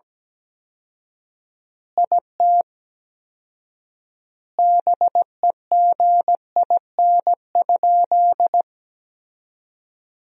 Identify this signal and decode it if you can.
Morse code, heard as a keyed tone, about 17 wpm, 700 Hz. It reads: IT BEGIN?
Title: 4x Word-Spacing - Sentences from Top 300 Words - 17wpm